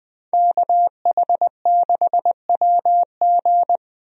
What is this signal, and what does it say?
Morse code, heard as a keyed tone, about 21 wpm, 700 Hz. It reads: KH6WG
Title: DXCC-100 Call Signs - 5 Characters - 20wpm